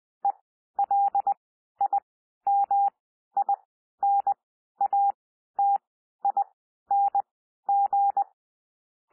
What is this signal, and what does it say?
Morse code, heard as a keyed tone, about 20 wpm, spaced as for 12 wpm, 800 Hz. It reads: ELIMINATING
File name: cwmp3 / P1 SOUND PASSAGE 20 wpm